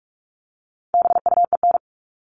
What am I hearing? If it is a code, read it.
Morse code, heard as a keyed tone, about 44 wpm, 700 Hz. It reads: BUEN
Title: All Cut Numbers - 4-Digit 45wpm